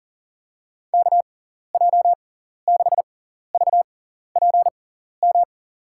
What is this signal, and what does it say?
Morse code, heard as a keyed tone, about 40 wpm, spaced as for 14 wpm, 700 Hz. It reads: KJ6VPM
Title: USA Call Signs - 6 Characters - 6x Character Spacing - 40wpm